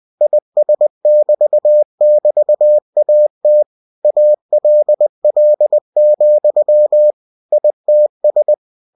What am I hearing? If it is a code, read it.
Morse code, heard as a keyed tone, about 20 wpm, 600 Hz. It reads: IS<BT><BT>AT ALL, ITS
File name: cw-books / winnie-the-pooh / winnie-20wpm-page-0079